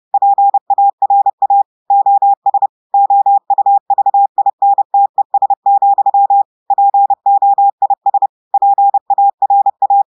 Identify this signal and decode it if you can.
Morse code, heard as a keyed tone, about 30 wpm, 800 Hz. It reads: PARA OS OUVINTES, POIS PARA
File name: portugais texte en clair 2 vitesse 30 mots minutes